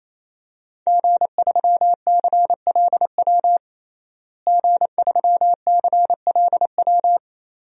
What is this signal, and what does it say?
Morse code, heard as a keyed tone, about 28 wpm, 700 Hz. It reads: G3CLW G3CLW
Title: DXCC-100 Call Signs - 5 Characters - Repeated 3x - 28wpm